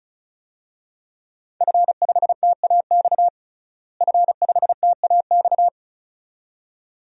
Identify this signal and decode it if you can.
Morse code, heard as a keyed tone, about 36 wpm, 700 Hz. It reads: F5TAX F5TAX
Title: DXCC-100 Call Signs - 5 Characters - Repeated 2x - 35wpm